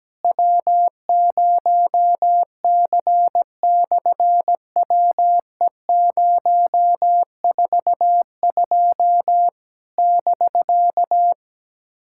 Morse code, heard as a keyed tone, about 17 wpm, 700 Hz. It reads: W0C/WE042 <BK>
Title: SOTA QSO Practice 17wpm